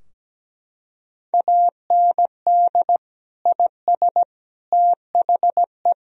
Morse code, heard as a keyed tone, about 17 wpm, 700 Hz. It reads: AND IS THE
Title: Sentences from Top 1000 Words - Review Part 3 - 17wpm